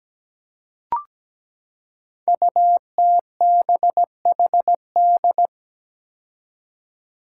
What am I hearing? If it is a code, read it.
Morse code, heard as a keyed tone, about 17 wpm, 700 Hz. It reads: UTBHD
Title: Five Challenging Letter-Number Combinations 17wpm